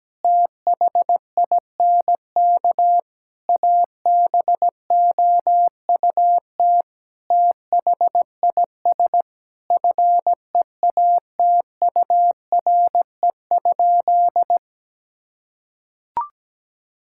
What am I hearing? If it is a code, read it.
Morse code, heard as a keyed tone, about 17 wpm, 700 Hz. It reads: THINK ABOUT THIS FEATURE?
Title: Sentences from Top 1000 Words - Review Part 5 - 17wpm